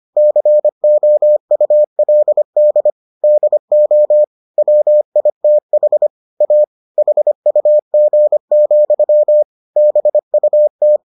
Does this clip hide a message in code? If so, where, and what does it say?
Yes: Morse code from 0.2 to 11.0 s, heard as a keyed tone, about 25 wpm, 600 Hz. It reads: COULD DO WITH A HUG, BUT